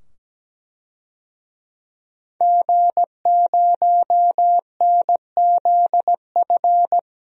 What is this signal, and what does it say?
Morse code, heard as a keyed tone, about 17 wpm, 700 Hz. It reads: G0NZF
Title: DXCC-100 Call Signs - 5 Characters - 17wpm